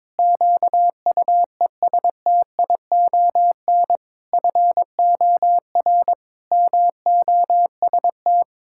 Morse code, heard as a keyed tone, about 22 wpm, 700 Hz. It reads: QUESTION FOR MOST
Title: Pangram Challenge v2 - 22wpm